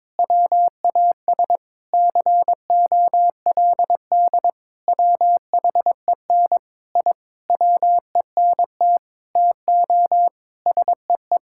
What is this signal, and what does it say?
Morse code, heard as a keyed tone, about 22 wpm, 700 Hz. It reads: WAS COLD WHEN I WENT TO SEE
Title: Sentences from Top 1000 Words - Review Part 5 - 22wpm